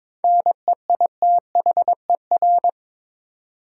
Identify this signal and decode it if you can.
Morse code, heard as a keyed tone, about 22 wpm, 700 Hz. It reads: NEITHER
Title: Seven Letter Words - 22wpm